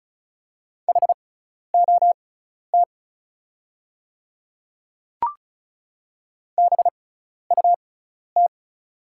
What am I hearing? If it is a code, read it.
Morse code, heard as a keyed tone, about 36 wpm, spaced as for 12 wpm, 690 Hz. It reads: HOT BUT
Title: 6x Character Spacing - Top 100 Words - 35wpm